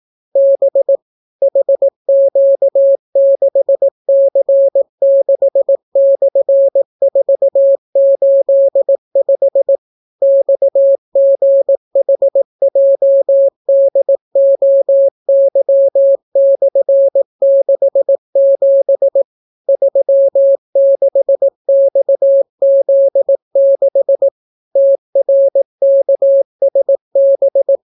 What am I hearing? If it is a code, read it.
Morse code, heard as a keyed tone, about 18 wpm, 550 Hz. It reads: B HQ6C6/485 XGHJDOY/67 36XZ6 TRKSB